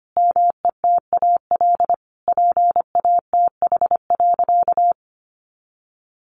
Morse code, heard as a keyed tone, about 25 wpm, 700 Hz. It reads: METAL PATH.
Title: A Sound Of Thunder 25wpm